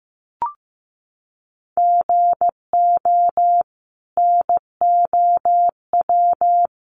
Morse code, heard as a keyed tone, about 15 wpm, 700 Hz. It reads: GO NOW
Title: Sentences from Top 1500 Words 15wpm